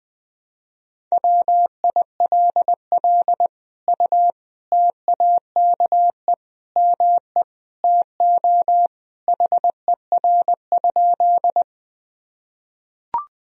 Morse code, heard as a keyed tone, about 20 wpm, 700 Hz. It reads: WILL U TAKE ME TO HER?
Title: Sentences from Top 200 Words and Abbreviations 20wpm